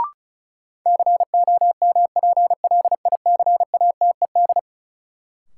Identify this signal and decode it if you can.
Morse code, heard as a keyed tone, about 34 wpm, 700 Hz. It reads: COMPLICATED
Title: Sentences from Top 1200 Words - New Word Drill - 35wpm